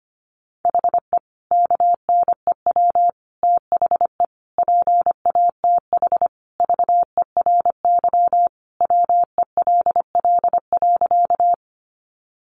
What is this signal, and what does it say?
Morse code, heard as a keyed tone, about 25 wpm, 700 Hz. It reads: HE KNEW THE PATH VERY WELL.
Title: Sentences from Top 1400 Words 25wpm